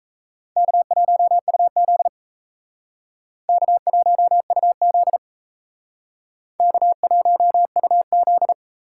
Morse code, heard as a keyed tone, about 36 wpm, 700 Hz. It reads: K1UZ K1UZ K1UZ